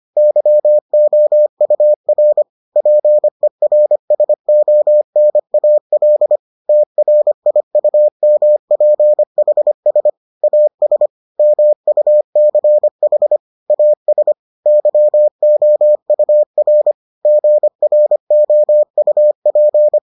Morse code, heard as a keyed tone, about 25 wpm, 600 Hz. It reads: YOUR PERSONAL TRIUMPHS AS MUCH AS YOUR GROUP